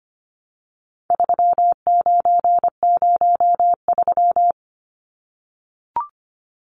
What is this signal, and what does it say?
Morse code, heard as a keyed tone, about 25 wpm, 700 Hz. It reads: 3903
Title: Numbers - 4-Digit 25wpm